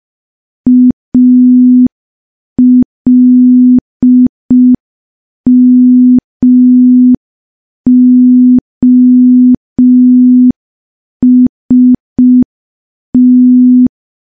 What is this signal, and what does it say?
Morse code, heard as a keyed tone, about 5 wpm, 250 Hz. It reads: ALMOST
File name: morse / morse-training-05WPM